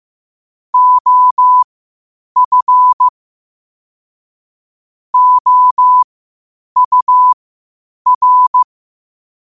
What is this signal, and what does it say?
Morse code, heard as a keyed tone, about 15 wpm, spaced as for 8 wpm, 1000 Hz. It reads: OF OUR
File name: qotdfiles / 8wpm